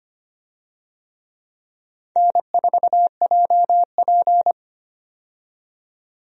Morse code, heard as a keyed tone, about 25 wpm, 700 Hz. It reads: N4JP
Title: USA Call Signs - 4 Characters - 25wpm